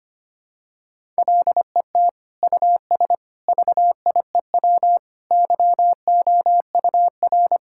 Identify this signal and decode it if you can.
Morse code, heard as a keyed tone, about 25 wpm, 700 Hz. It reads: LET US VIEW YOUR
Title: Sentences from Top 1000 Words - Review Part 6 - 25wpm